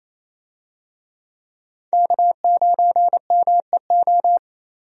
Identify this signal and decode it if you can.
Morse code, heard as a keyed tone, about 28 wpm, 700 Hz. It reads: K9MEO
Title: USA Call Signs - 5 Characters - 28wpm